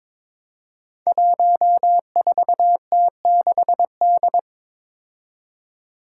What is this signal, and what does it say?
Morse code, heard as a keyed tone, about 22 wpm, 700 Hz. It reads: 14T6D
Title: Five Challenging Letter-Number Combinations 22wpm